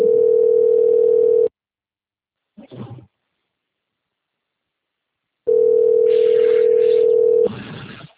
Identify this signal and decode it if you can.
Ringback tone (North America), 440 + 480 Hz.